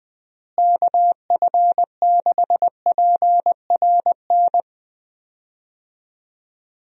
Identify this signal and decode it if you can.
Morse code, heard as a keyed tone, about 20 wpm, 700 Hz. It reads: KF6PRN
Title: USA Call Signs - 6 Characters - Repeated 2x - 20wpm